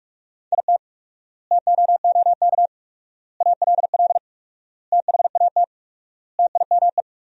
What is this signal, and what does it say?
Morse code, heard as a keyed tone, about 46 wpm, 700 Hz. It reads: IT TOOK ALL THAT TIME
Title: 4x Word-Spacing - Sentences from Top 300 Words - 45wpm